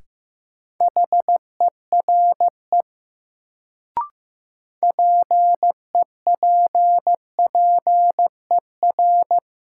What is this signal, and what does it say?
Morse code, heard as a keyed tone, about 15 wpm, 700 Hz. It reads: HERE PEPPER
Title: Sets of 5 Words - New Word Drill - 15wpm